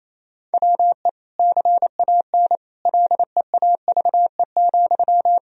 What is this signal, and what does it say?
Morse code, heard as a keyed tone, about 28 wpm, 700 Hz. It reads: WE CAN LEAVE,
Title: Sentences from Top 1000 Words - Review Part 4 - 28wpm